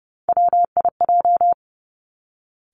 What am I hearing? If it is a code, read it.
Morse code, heard as a keyed tone, about 31 wpm, 700 Hz. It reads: WIJ
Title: Three Challenging Letter-Number Combinations 30wpm